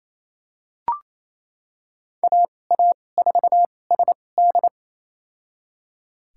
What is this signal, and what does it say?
Morse code, heard as a keyed tone, about 28 wpm, spaced as for 20 wpm, 700 Hz. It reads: AA4SD
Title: USA Call Signs - 5 Characters - 2x Character Spacing - 28wpm